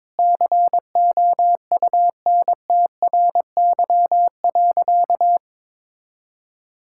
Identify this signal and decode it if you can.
Morse code, heard as a keyed tone, about 22 wpm, 700 Hz. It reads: COUNTRY.
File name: practice / 22wpm / Sentences with 4-Word Phrases - 22wpm